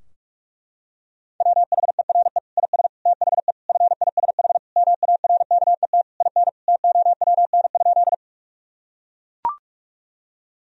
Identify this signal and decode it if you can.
Morse code, heard as a keyed tone, about 41 wpm, 700 Hz. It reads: WHERE IS THE FISH MARKET IN TOWN?
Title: Sentences from Top 1000 Words - Review Part 3 - 45wpm